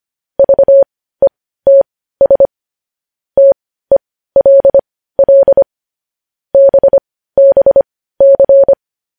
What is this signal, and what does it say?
Morse code, heard as a keyed tone, about 25 wpm, spaced as for 15 wpm, 550 Hz. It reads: VETS TELL BBC
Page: news headlines in morse code at 15 wpm